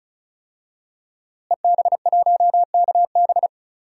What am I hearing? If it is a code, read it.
Morse code, heard as a keyed tone, about 36 wpm, 700 Hz. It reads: EB1KB
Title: DXCC-100 Call Signs - 5 Characters - Repeated 3x - 35wpm